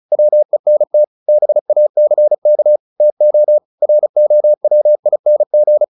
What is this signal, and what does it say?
Morse code, heard as a keyed tone, about 34 wpm, 600 Hz. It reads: WENT BACK TO ROWING
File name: the old man and the sea / the old man and the sea 1 35wpm